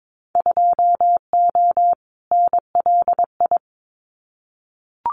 Morse code, heard as a keyed tone, about 22 wpm, 700 Hz. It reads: 2O NLI
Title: Winter Field Day Exchanges 22wpm